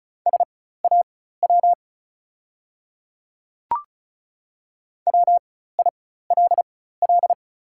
Morse code, heard as a keyed tone, about 36 wpm, spaced as for 16 wpm, 690 Hz. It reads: SAW WILL
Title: 4x Character Spacing - Top 500 Words - 35wpm